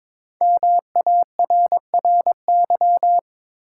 Morse code, heard as a keyed tone, about 23 wpm, 700 Hz. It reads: MARRY
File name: 5 Letter Words - First 3 Letters Given - 22wpm